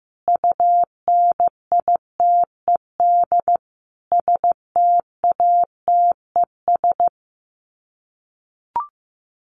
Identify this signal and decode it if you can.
Morse code, heard as a keyed tone, about 15 wpm, 700 Hz. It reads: UNITED STATES
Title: Sets of 5 Words Encore2 15wpm